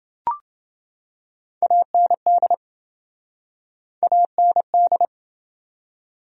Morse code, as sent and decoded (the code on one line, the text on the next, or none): .- -. -.. / .- -. -..
AND AND